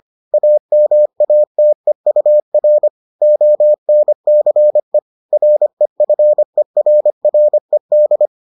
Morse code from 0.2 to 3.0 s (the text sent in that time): AMATEUR